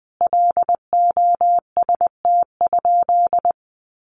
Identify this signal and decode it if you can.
Morse code, heard as a keyed tone, about 20 wpm, 700 Hz. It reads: LOST?
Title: Sentences from Top 1400 Words 20wpm